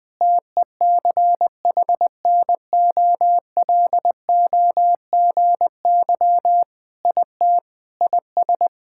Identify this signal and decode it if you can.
Morse code, heard as a keyed tone, about 20 wpm, 700 Hz. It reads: TECHNOLOGY IT IS
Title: Sentences with 3-Word Phrases - Spoken Beginning - 20wpm